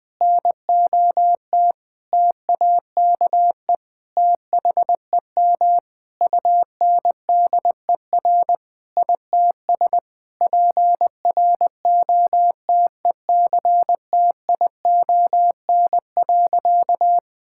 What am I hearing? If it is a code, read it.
Morse code, heard as a keyed tone, about 20 wpm, 700 Hz. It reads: NOT TAKE THEM UNDER ITS PROTECTION.
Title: Sentences from Top 1600 Words 20wpm